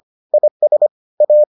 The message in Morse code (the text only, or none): IS A